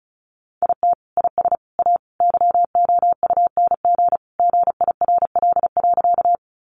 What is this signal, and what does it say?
Morse code, heard as a keyed tone, about 34 wpm, 700 Hz. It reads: IT IS A YOUNG GIRL.
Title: Sentences from Top 400 Words 35wpm